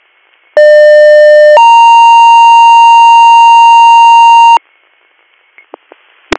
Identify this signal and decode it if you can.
Two-tone page, heard as a one-second tone, then a three-second tone, about 600 then 930 Hz. It is heard from 0.6 to 4.6 s.